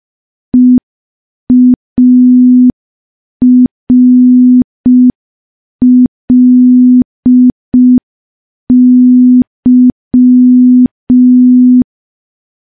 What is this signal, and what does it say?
Morse code, heard as a keyed tone, about 5 wpm, 250 Hz. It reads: EARLY
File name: morse / morse-training-05WPM